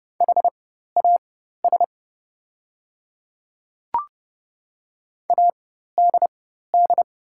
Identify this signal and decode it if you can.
Morse code, heard as a keyed tone, about 30 wpm, spaced as for 14 wpm, 700 Hz. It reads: HAS ADD
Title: 4x Character Spacing - Top 200 Words - 30wpm